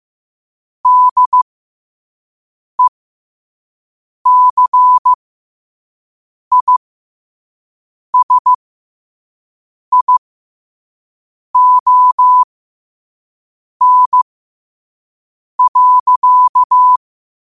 Morse code, heard as a keyed tone, about 15 wpm, spaced as for 5 wpm, 1000 Hz. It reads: DECISION.
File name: qotdfiles / 5wpm